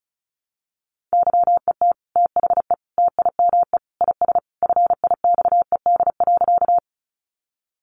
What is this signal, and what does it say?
Morse code, heard as a keyed tone, about 34 wpm, 700 Hz. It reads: YET THE TIME IS FIXED.
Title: Sentences from Top 1100 Words 35wpm